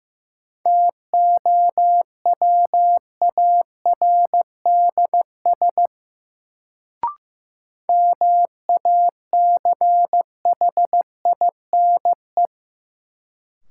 Morse code, heard as a keyed tone, about 15 wpm, 700 Hz. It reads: TOWARDS MACHINE
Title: Sentences from Top 1000 Words - Review Part 2 - New Word Drill - 15wpm